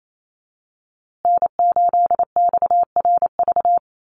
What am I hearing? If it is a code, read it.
Morse code, heard as a keyed tone, about 28 wpm, 700 Hz. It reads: N8XRV
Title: USA Call Signs Repeated 3x 28wpm